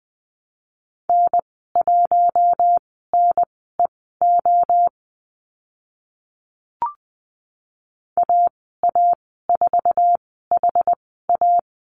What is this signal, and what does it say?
Morse code, heard as a keyed tone, about 20 wpm, spaced as for 15 wpm, 700 Hz. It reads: N1NEO AA4HA
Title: 062 Lesson 11g - Letter h - Call Signs 20wpm